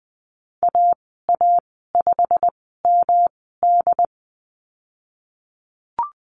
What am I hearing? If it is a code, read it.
Morse code, heard as a keyed tone, about 20 wpm, spaced as for 15 wpm, 700 Hz. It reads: AA5MD
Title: 128 Lesson 21g - Number 6 - Call Signs 20wpm